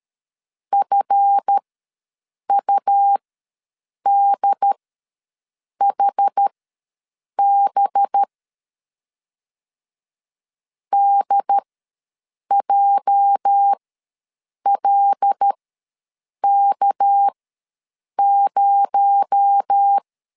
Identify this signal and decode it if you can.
Morse code, heard as a keyed tone, about 13 wpm, spaced as for 7 wpm, 800 Hz. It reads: FUDHB DJLK0
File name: LEZIONI / 8 YJ45 / 8 YJ ws 12wpm 3342